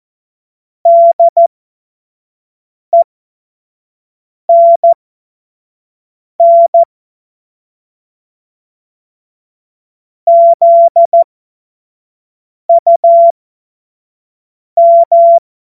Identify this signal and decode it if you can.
Morse code, heard as a keyed tone, about 14 wpm, spaced as for 5 wpm, 680 Hz. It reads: DENN ZUM
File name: Morsekurs / Witze / Witz 15. August 2023 25 F70